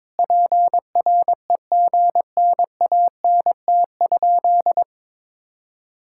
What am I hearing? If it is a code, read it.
Morse code, heard as a keyed tone, about 21 wpm, 700 Hz. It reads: PREGNANT?
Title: Sentences from Top 1000 Words - Review Part 6 - 22wpm